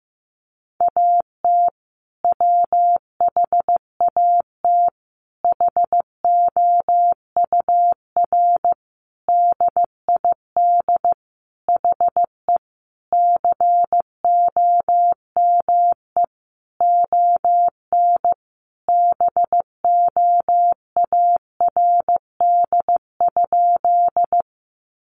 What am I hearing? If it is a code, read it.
Morse code, heard as a keyed tone, about 15 wpm, 700 Hz. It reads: AT WHAT HOUR DID HE COME ON BOARD?
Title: Sentences from Top 700 Words 15wpm